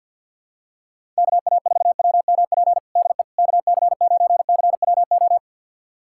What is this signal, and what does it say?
Morse code, heard as a keyed tone, about 47 wpm, 700 Hz. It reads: KA4WMP DE KC9CWO
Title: USA Callsign DE Callsign - 6 Characters - 50wpm